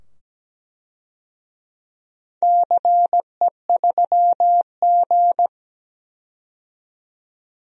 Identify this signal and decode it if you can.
Morse code, heard as a keyed tone, about 17 wpm, 700 Hz. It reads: CE3G
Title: DXCC-100 Call Signs - 4 Characters - 17wpm